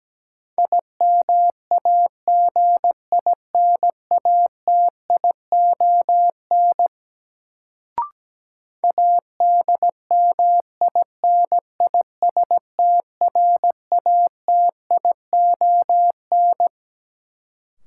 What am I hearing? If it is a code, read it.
Morse code, heard as a keyed tone, about 17 wpm, 700 Hz. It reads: IMAGINATION ADMINISTRATION
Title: Sentences from Top 1300 Words - New Word Drill - 17wpm